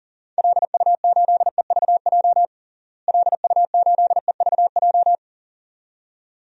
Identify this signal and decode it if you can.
Morse code, heard as a keyed tone, about 40 wpm, 700 Hz. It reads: LU8EVJ LU8EVJ
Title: DXCC-100 Call Signs - 6 Characters - Repeated 2x - 40wpm